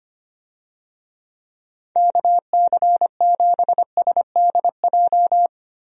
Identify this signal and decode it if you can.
Morse code, heard as a keyed tone, about 25 wpm, 700 Hz. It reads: KC7SDJ
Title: USA Call Signs - 6 Characters - Repeated 3x - 25wpm